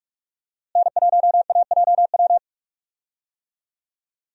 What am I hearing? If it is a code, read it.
Morse code, heard as a keyed tone, about 46 wpm, 700 Hz. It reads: N1AJW
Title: USA Call Signs - 5 Characters - Repeated 2x - 45wpm